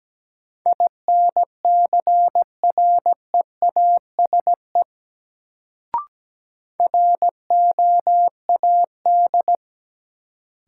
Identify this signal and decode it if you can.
Morse code, heard as a keyed tone, about 17 wpm, 700 Hz. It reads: INCREASE ROAD
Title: Sets of 5 Words - New Word Drill - 17wpm